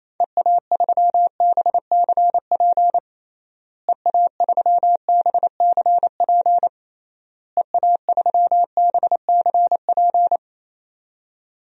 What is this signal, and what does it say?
Morse code, heard as a keyed tone, about 28 wpm, 700 Hz. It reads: EA3BCP EA3BCP EA3BCP
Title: DXCC-100 Call Signs - 6 Characters - Repeated 3x - 28wpm